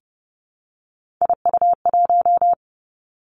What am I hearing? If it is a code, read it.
Morse code, heard as a keyed tone, about 30 wpm, 700 Hz. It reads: IU1
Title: Three Challenging Letter-Number Combinations 30wpm